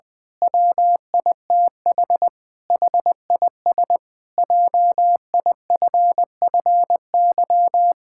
Morse code, heard as a keyed tone, about 20 wpm, 700 Hz. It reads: WITH HIS JIFFY
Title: Pangram Challenge v2 - 20wpm